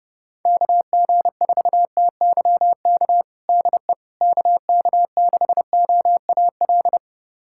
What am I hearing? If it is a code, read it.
Morse code, heard as a keyed tone, about 30 wpm, 700 Hz. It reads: KG4TYK DE KK6OAL
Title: USA Callsign DE Callsign - 6 Characters - 30wpm